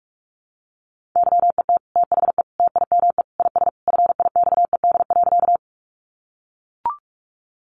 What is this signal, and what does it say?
Morse code, heard as a keyed tone, about 44 wpm, 700 Hz. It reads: YET THE TIME IS FIXED.
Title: Sentences from Top 1100 Words 45wpm